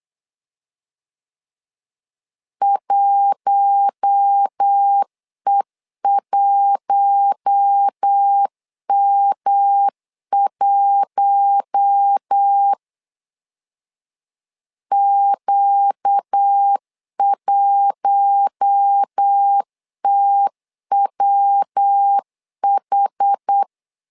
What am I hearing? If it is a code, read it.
Morse code, heard as a keyed tone, about 8 wpm, 800 Hz. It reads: 1E1M1 Q1TWH
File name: LEZIONI / 4 Q1 8wpm 3035